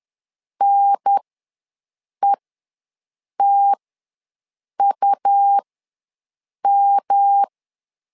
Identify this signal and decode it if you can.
Morse code, heard as a keyed tone, about 10 wpm, spaced as for 6 wpm, 800 Hz. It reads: NETUM
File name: LEZIONI / 3 MWRO / 3 M ws 10wpm 2121